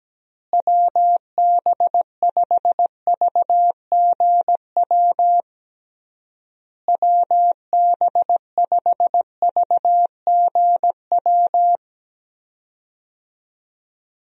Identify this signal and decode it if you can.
Morse code, heard as a keyed tone, about 17 wpm, 700 Hz. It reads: WB5VGW WB5VGW
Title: USA Call Signs - 6 Characters - Repeated 3x - 17wpm